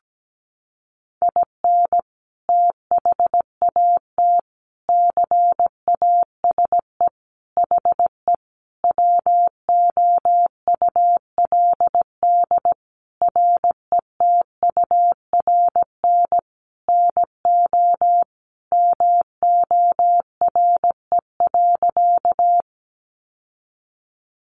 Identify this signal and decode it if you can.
Morse code, heard as a keyed tone, about 17 wpm, 700 Hz. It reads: IN THAT CASE HE WOULD RETURN NO MORE.
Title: Sentences from Top 800 Words 17wpm